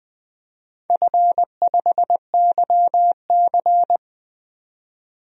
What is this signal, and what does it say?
Morse code, heard as a keyed tone, about 20 wpm, 700 Hz. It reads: F5YC
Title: DXCC-100 Call Signs - 4 Characters - 20wpm